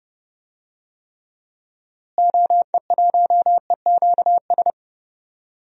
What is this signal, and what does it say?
Morse code, heard as a keyed tone, about 30 wpm, 700 Hz. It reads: OE1EQS
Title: DXCC-100 Call Signs - 6 Characters - Repeated 3x - 30wpm